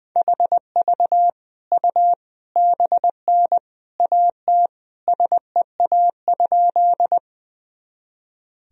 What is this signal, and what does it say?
Morse code, heard as a keyed tone, about 20 wpm, 700 Hz. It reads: HV U BN AT SEA?
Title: Sentences from Top 300 Words and Abbreviations 20wpm